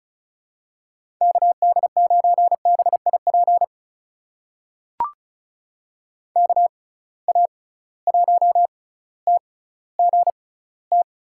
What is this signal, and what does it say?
Morse code, heard as a keyed tone, about 36 wpm, spaced as for 18 wpm, 700 Hz. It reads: KD9BIP KA1TGT
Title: USA Call Signs - 6 Characters - 6x Character Spacing - 35wpm